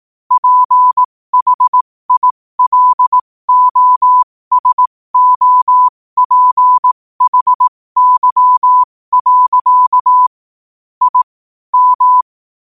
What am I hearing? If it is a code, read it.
Morse code, heard as a keyed tone, about 18 wpm, spaced as for 15 wpm, 1000 Hz. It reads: PHILOSOPHY. I M
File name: qotdfiles / 15wpm